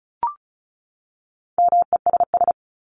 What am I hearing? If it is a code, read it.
Morse code, heard as a keyed tone, about 34 wpm, 700 Hz. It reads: MESS
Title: Top 1751-2000 Words 35wpm